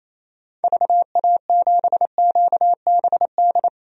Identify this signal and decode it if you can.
Morse code, heard as a keyed tone, about 28 wpm, 700 Hz. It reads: VA7QBD
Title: DXCC-100 Call Signs - 6 Characters - Repeated 3x - 28wpm